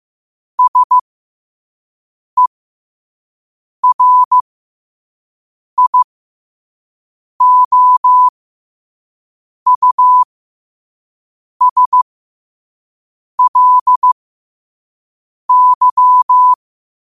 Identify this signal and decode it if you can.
Morse code, heard as a keyed tone, about 15 wpm, spaced as for 5 wpm, 1000 Hz. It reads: SERIOUSLY
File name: qotdfiles / 5wpm